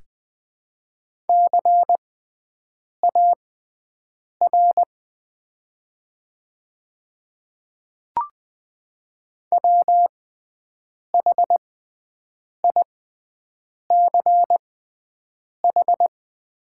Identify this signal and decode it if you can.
Morse code, heard as a keyed tone, about 20 wpm, spaced as for 7 wpm, 700 Hz. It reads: CAR WHICH
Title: 6x Character Spacing - Top 300 Words - 20wpm